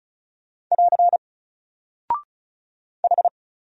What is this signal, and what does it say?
Morse code, heard as a keyed tone, about 36 wpm, 700 Hz. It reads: <AR> H
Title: Single Character (Letter:Number:Punctuation:Prosign) 35wpm